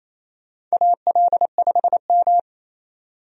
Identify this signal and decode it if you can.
Morse code, heard as a keyed tone, about 28 wpm, 700 Hz. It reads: AL5M
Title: USA Call Signs - 4 Characters - 28wpm